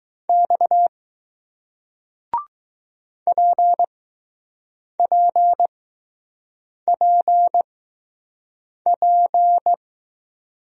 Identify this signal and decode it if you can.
Morse code, heard as a keyed tone, about 20 wpm, spaced as for 6 wpm, 700 Hz. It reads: X PPPP